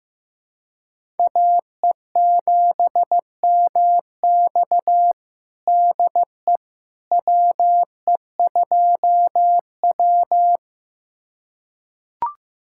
Morse code, heard as a keyed tone, about 16 wpm, 700 Hz. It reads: AE7MX DE WE2W
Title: USA Callsign DE Callsign 15wpm